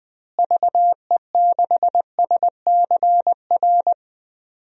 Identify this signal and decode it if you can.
Morse code, heard as a keyed tone, about 20 wpm, 700 Hz. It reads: VE6SCR
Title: DXCC-100 Call Signs - 6 Characters - Repeated 3x - 20wpm